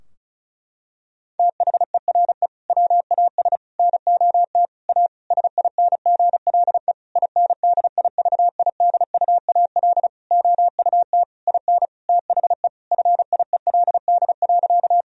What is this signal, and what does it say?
Morse code, heard as a keyed tone, about 36 wpm, 700 Hz. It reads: THERE WAS NOT A SINGLE INDIVIDUAL OUT IN THE FIELD.
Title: Sentences from Top 1000 Words - Review Part 4 - 35wpm